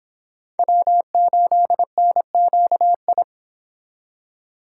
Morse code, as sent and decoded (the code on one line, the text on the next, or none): .-- ---.. -. --.- ..
W8NQI